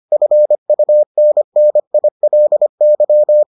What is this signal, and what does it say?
Morse code, heard as a keyed tone, about 25 wpm, 600 Hz. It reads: FUNNILY